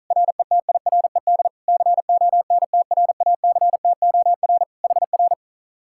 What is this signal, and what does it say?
Morse code, heard as a keyed tone, about 44 wpm, 700 Hz. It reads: RETIRED CONTRACTOR HR